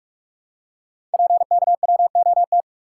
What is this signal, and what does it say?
Morse code, heard as a keyed tone, about 44 wpm, 690 Hz. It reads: PKWOT
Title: Code Groups - 5 Letters - 45wpm